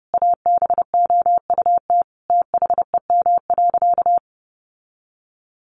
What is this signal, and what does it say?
Morse code, heard as a keyed tone, about 30 wpm, 700 Hz. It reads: ABOUT THEM.
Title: Sentences from Top 800 Words 30wpm